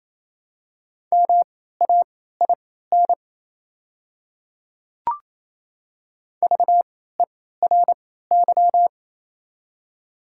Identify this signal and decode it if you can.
Morse code, heard as a keyed tone, about 28 wpm, spaced as for 16 wpm, 700 Hz. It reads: MAIN VERY
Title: 3x Character Spacing - Top 300 Words - 28wpm